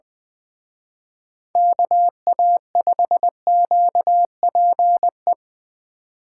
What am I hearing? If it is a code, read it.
Morse code, heard as a keyed tone, about 23 wpm, 700 Hz. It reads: KA5QPE
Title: Speed Racing - USA 2x3 Call Signs 20wpm